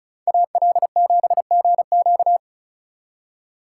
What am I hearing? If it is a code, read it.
Morse code, heard as a keyed tone, about 36 wpm, 700 Hz. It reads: AL7GQ
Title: Speed Racing - USA Call Signs 35wpm